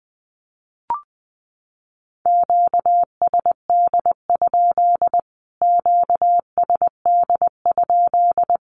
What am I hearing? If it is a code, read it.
Morse code, heard as a keyed tone, about 20 wpm, 700 Hz. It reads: QSD? QSD?
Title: Q-Codes Repeated 3x 20wpm